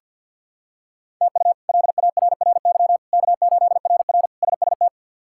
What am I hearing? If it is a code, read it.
Morse code, heard as a keyed tone, about 50 wpm, 700 Hz. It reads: TU LARRY K8RR SST
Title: K1USN SST QSO Practice - 50wpm